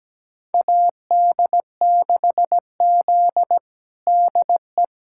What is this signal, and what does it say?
Morse code, heard as a keyed tone, about 17 wpm, 700 Hz. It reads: AD6Z DE
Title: USA Callsign DE Callsign - 4 Characters - 17wpm